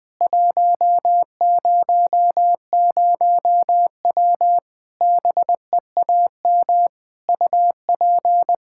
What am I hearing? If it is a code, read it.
Morse code, heard as a keyed tone, about 21 wpm, 700 Hz. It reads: 100W BEAM UP
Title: K1USN SST QSO Practice 20wpm